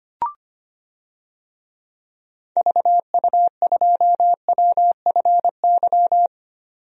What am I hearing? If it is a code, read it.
Morse code, heard as a keyed tone, about 25 wpm, 700 Hz. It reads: VU2WFY
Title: DXCC-100 Call Signs - 6 Characters - 25wpm